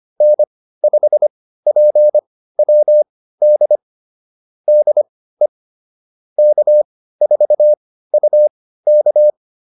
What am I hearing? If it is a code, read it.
Morse code, heard as a keyed tone, about 25 wpm, spaced as for 15 wpm, 600 Hz. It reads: N5PWD DE K4UK
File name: QSO208 15